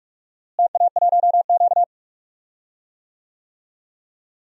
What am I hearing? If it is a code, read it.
Morse code, heard as a keyed tone, about 47 wpm, 700 Hz. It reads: TA1Q
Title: DXCC-100 Call Signs - 4 Characters - Repeated 2x - 45wpm